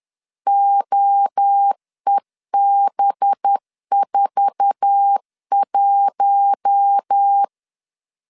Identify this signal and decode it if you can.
Morse code, heard as a keyed tone, about 10 wpm, 800 Hz. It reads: OEB41
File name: LEZIONI / 8 YJ45 / 8 YJ4 10wpm 2633